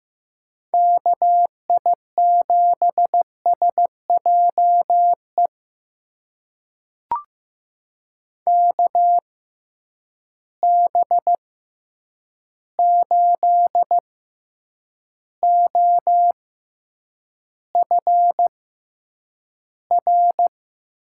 Morse code, heard as a keyed tone, about 15 wpm, spaced as for 8 wpm, 700 Hz. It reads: KI7SJE KB8OFR
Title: USA Call Signs - 6x Character Spacing - 15wpm